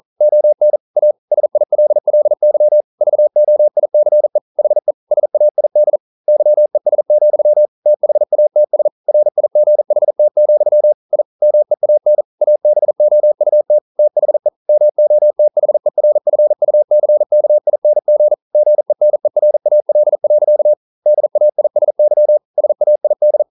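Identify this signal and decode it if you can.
Morse code, heard as a keyed tone, about 42 wpm, 600 Hz. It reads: ON A SILLY VOICE HE SAID YES, THATS RIGHT, I MEAN ABOUT THE MOTHERFUCKING GENERAL. DAISY SAID
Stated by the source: Morse code practice at 41 wpm, from CW med Gnister.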